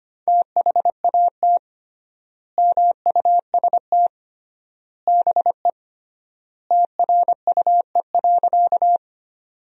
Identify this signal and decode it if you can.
Morse code, heard as a keyed tone, about 25 wpm, 700 Hz. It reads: THAT MUST BE TRUE.
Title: 3x Word-Spacing - Sentences from Top 400 Words - 25wpm